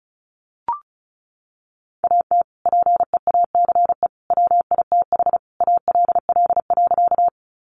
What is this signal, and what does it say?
Morse code, heard as a keyed tone, about 34 wpm, 700 Hz. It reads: AT PEACE WITH ALL.
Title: Sentences from Top 1500 Words 35wpm